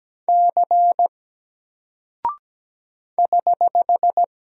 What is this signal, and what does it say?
Morse code, heard as a keyed tone, about 17 wpm, 700 Hz. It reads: C <HH>
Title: Single Character (Letter:Number:Punctuation:Prosign) Alt 17wpm